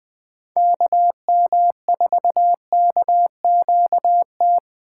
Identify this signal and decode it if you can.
Morse code, heard as a keyed tone, about 20 wpm, 700 Hz. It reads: KM4KQT
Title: Speed Racing - USA 2x3 Call Signs 17wpm